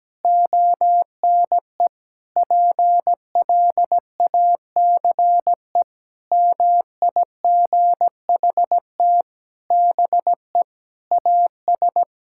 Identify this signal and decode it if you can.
Morse code, heard as a keyed tone, about 17 wpm, 700 Hz. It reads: ONE PLACE MIGHT BE AS
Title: Sentences from Top 1600 Words 17wpm